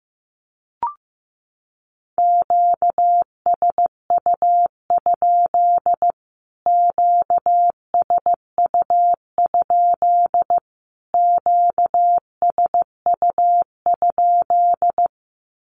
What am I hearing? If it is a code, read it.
Morse code, heard as a keyed tone, about 15 wpm, 700 Hz. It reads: QSU? QSU? QSU?
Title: Q-Codes Repeated 3x 15wpm